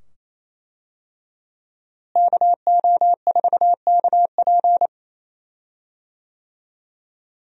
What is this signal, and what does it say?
Morse code, heard as a keyed tone, about 28 wpm, 700 Hz. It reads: KO4KP
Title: USA Call Signs - 5 Characters - Repeated 2x - 28wpm